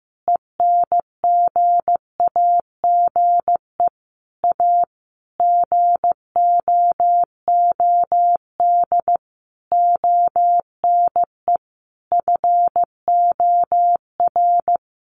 Morse code, heard as a keyed tone, about 15 wpm, 700 Hz. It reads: ENGAGE A GOOD ONE FOR
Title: Sentences from Top 1500 Words 15wpm